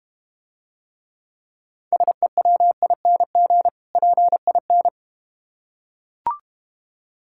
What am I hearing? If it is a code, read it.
Morse code, heard as a keyed tone, about 31 wpm, 700 Hz. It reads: SEWING PIN
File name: practice / 25wpm / Difficult Words - Speed-Racing with Hints - 25wpm